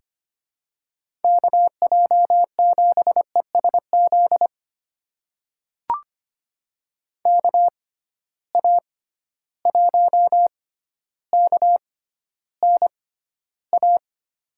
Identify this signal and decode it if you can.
Morse code, heard as a keyed tone, about 25 wpm, spaced as for 13 wpm, 700 Hz. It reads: KJ7ESZ KA1KNA
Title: USA Call Signs - 6 Characters - 6x Character Spacing - 25wpm